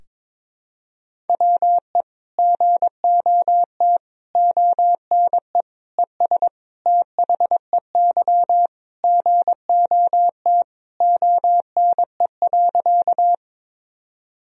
Morse code, heard as a keyed tone, about 22 wpm, 700 Hz. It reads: WE GOT ONE ES THEY GOT ONE.
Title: Sentences from Top 300 Words and Abbreviations 22wpm